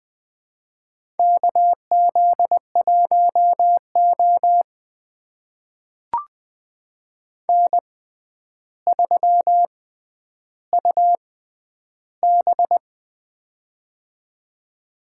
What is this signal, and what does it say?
Morse code, heard as a keyed tone, about 20 wpm, spaced as for 10 wpm, 700 Hz. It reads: KZ1O N3UB
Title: USA Call Signs - 4 Characters - 6x Character Spacing - 20wpm